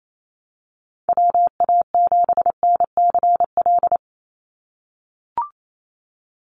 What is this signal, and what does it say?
Morse code, heard as a keyed tone, about 28 wpm, 700 Hz. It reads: WA7NCL
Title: NA SOTA Activator Callsigns 28wpm